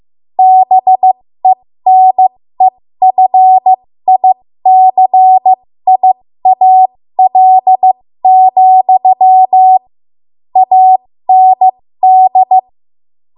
Morse code, heard as a keyed tone, about 15 wpm, spaced as for 13 wpm, 750 Hz. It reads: BENEFICIAL, AND
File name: Morse / 13WPM